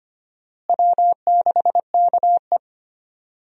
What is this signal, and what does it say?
Morse code, heard as a keyed tone, about 25 wpm, 700 Hz. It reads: W6KE
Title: USA Call Signs - 4 Characters - Repeated 3x - 25wpm